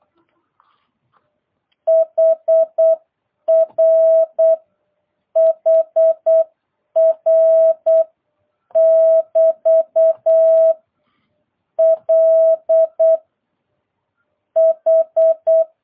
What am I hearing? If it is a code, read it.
Morse code, heard as a keyed tone, about 8 wpm, 650 Hz. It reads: HR HR <BT> L H